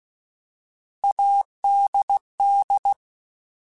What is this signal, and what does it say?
Morse code, heard as a keyed tone, about 16 wpm, 800 Hz. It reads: ADD